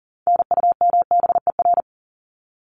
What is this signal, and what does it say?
Morse code, heard as a keyed tone, about 40 wpm, 700 Hz. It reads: NUMBER
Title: ES:S Suffix - Top 100 Words with Base Words 40wpm